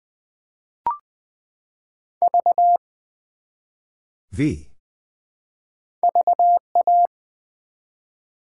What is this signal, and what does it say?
Morse code, heard as a keyed tone, about 20 wpm, 700 Hz. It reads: V VA
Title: Build A Word - 1 Letter At A Time - 5-Letter Words - 20wpm